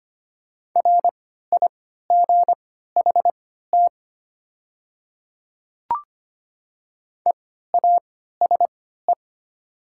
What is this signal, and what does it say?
Morse code, heard as a keyed tone, about 25 wpm, spaced as for 14 wpm, 700 Hz. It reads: RIGHT EASE
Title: 3x Character Spacing - Top 300 Words - 25wpm